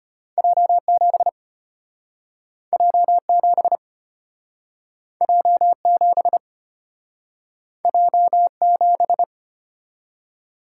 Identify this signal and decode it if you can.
Morse code, heard as a keyed tone, about 34 wpm, 700 Hz. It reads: J7 J7 J7 J7